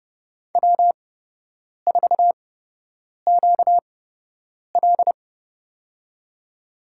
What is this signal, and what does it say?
Morse code, heard as a keyed tone, about 31 wpm, spaced as for 8 wpm, 700 Hz. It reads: W4QL
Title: USA Call Signs - 4 Characters - 8x Character Spacing - 30wpm